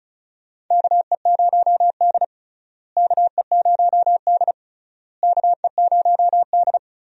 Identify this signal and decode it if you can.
Morse code, heard as a keyed tone, about 36 wpm, 700 Hz. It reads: KE0D KE0D KE0D
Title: USA Call Signs - 4 Characters - Repeated 3x - 35wpm